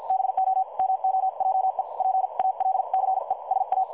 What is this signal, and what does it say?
Morse code, heard as a keyed tone, about 50 wpm, spaced as for 43 wpm, 760 Hz. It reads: BO MOJE PALCE SA
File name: dlazaawansowanych / nagraniazpasma / websdr recording 3565.0kHz